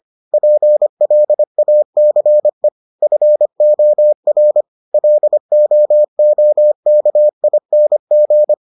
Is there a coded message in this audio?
Yes: Morse code from 0.3 to 8.5 s, heard as a keyed tone, about 25 wpm, 600 Hz. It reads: PLACE FOR LOOKING